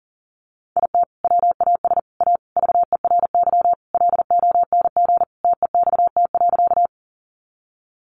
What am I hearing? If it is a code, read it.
Morse code, heard as a keyed tone, about 37 wpm, 700 Hz. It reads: IT WAS A VERY LONG TEXT.
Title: Sentences from Top 1200 Words 40wpm